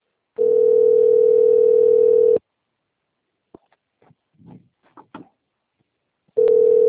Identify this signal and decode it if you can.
Ringback tone (North America), 440 + 480 Hz.
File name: rg-1600-unknown- - - .15553